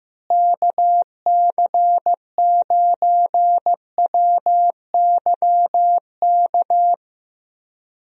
Morse code, heard as a keyed tone, about 15 wpm, 700 Hz. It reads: KC9WYK
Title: USA Call Signs - 6 Characters - Repeated 2x - 15wpm